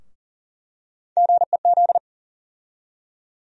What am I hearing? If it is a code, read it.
Morse code, heard as a keyed tone, about 40 wpm, 700 Hz. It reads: GEZ
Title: USA Call Sign Postfixes - 3 Letters 40wpm